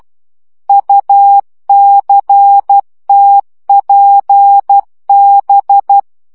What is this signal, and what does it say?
Morse code, heard as a keyed tone, about 12 wpm, 800 Hz. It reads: UCTPB
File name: morze vizsga 2011-3 10